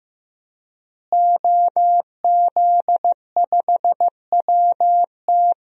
Morse code, heard as a keyed tone, about 15 wpm, 700 Hz. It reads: OZ5WT
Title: DXCC-100 Call Signs - 5 Characters - Repeated 3x - 15wpm